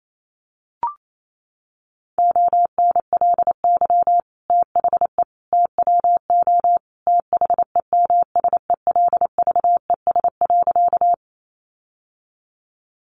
Morse code, heard as a keyed tone, about 28 wpm, 700 Hz. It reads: ONLY THE TWO THEMSELVES.
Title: Sentences from Top 800 Words 28wpm